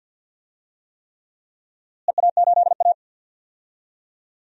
Morse code, heard as a keyed tone, about 49 wpm, 700 Hz. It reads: EA8A
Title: DXCC-100 Call Signs - 4 Characters - Repeated 3x - 50wpm